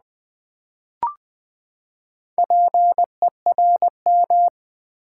Morse code, heard as a keyed tone, about 20 wpm, 700 Hz. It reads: PERM
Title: Build A Word - 4 Letters At A Time - 10-Letter Words - 20wpm